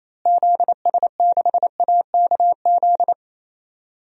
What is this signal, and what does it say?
Morse code, heard as a keyed tone, about 28 wpm, 700 Hz. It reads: ZS6AKZ
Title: DXCC-100 Call Signs - 6 Characters - Repeated 3x - 28wpm